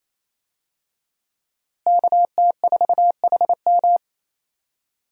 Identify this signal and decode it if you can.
Morse code, heard as a keyed tone, about 28 wpm, 700 Hz. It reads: KT4HM
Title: USA Call Signs - 5 Characters - Repeated 2x - 28wpm